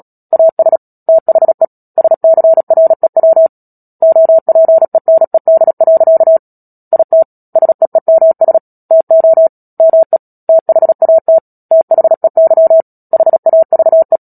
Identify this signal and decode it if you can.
Morse code, heard as a keyed tone, about 37 wpm, 650 Hz. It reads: AS THE SCREW OPENED. IT SEEMS TO ME THAT THEY HAVE
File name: Book Two / 10-Chapter 10 - Epilogue - 36 WPM